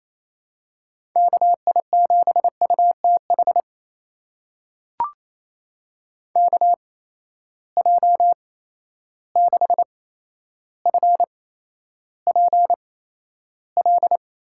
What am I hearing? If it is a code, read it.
Morse code, heard as a keyed tone, about 28 wpm, spaced as for 12 wpm, 700 Hz. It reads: KI7UTH KJ6FPL
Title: USA Call Signs - 6 Characters - 8x Character Spacing - 28wpm